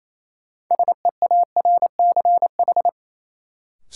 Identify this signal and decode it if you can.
Morse code, heard as a keyed tone, about 28 wpm, 700 Hz. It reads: SEARCH
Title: Sentences from Top 800 Words - New Word Drill - 28wpm